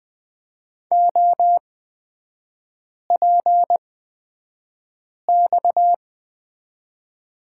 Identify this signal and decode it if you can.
Morse code, heard as a keyed tone, about 20 wpm, spaced as for 5 wpm, 700 Hz. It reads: OPX
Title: 229 Lesson 37e - Letter x - Three Characters 20wpm